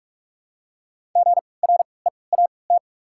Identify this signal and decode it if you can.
Morse code, heard as a keyed tone, about 47 wpm, spaced as for 26 wpm, 700 Hz. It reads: GREAT
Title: 3x Character Spacing - Top 300 Words - 45wpm